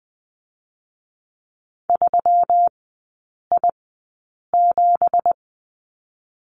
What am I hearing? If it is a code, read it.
Morse code, heard as a keyed tone, about 19 wpm, spaced as for 8 wpm, 700 Hz. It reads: 3I7
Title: 166 Lesson 27e - Number 7 - Three Characters 20wpm